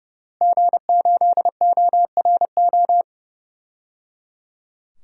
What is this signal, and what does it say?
Morse code, heard as a keyed tone, about 30 wpm, 700 Hz. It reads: G8ORO
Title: DXCC-100 Call Signs - 5 Characters - Repeated 3x - 30wpm